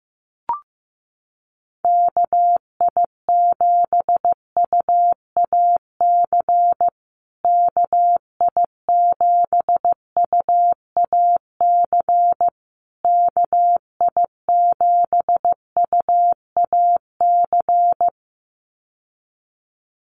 Morse code, heard as a keyed tone, about 15 wpm, 700 Hz. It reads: KI7UAC KI7UAC KI7UAC
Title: USA Call Signs Repeated 3x 15wpm